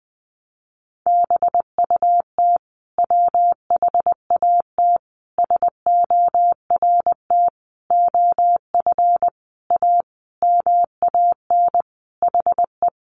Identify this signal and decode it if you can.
Morse code, heard as a keyed tone, about 20 wpm, 700 Hz. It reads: BUT WHAT SORT OF A MAN HE
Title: Sentences from Top 1300 Words 20wpm